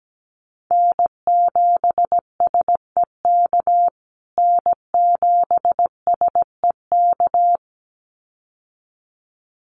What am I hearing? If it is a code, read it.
Morse code, heard as a keyed tone, about 17 wpm, 700 Hz. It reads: N7SEK N7SEK
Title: USA Call Signs Repeated 2x 17wpm